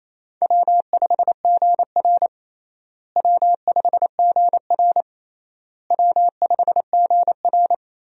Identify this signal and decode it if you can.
Morse code, heard as a keyed tone, about 28 wpm, 700 Hz. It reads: W5GR W5GR W5GR
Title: USA Call Signs - 4 Characters - Repeated 3x - 28wpm